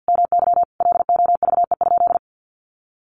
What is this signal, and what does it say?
Morse code, heard as a keyed tone, about 50 wpm, 700 Hz. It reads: MY LOVE?
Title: Sentences from Top 1500 Words 50wpm